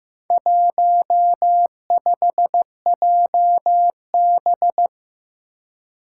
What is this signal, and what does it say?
Morse code, heard as a keyed tone, about 15 wpm, 700 Hz. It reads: 15JB
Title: Four Challenging Letter-Number Combinations 15wpm